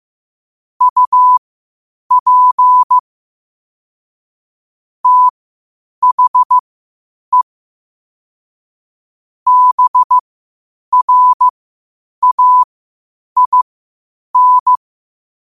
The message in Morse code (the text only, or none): UP THE BRAIN